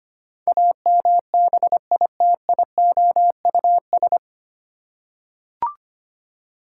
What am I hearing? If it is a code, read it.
Morse code, heard as a keyed tone, about 25 wpm, 700 Hz. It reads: AMBITIOUS
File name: Nine Letter Words - 25wpm